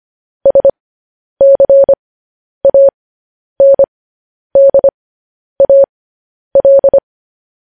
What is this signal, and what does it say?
Morse code, heard as a keyed tone, about 25 wpm, spaced as for 10 wpm, 550 Hz. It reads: SCANDAL
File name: podcast / News headlines-1x-25x10